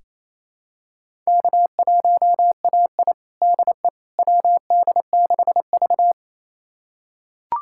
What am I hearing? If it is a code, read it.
Morse code, heard as a keyed tone, about 28 wpm, 700 Hz. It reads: K1AI DE WD6V
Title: USA Callsign DE Callsign - 4 Characters - 28wpm